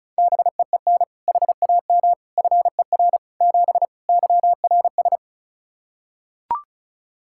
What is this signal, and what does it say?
Morse code, heard as a keyed tone, about 34 wpm, 700 Hz. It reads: BEEN HAM FER 7 YRS
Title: K1USN SST QSO Practice 35wpm